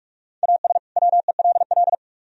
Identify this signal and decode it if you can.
Morse code, heard as a keyed tone, about 44 wpm, 700 Hz. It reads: AS WELL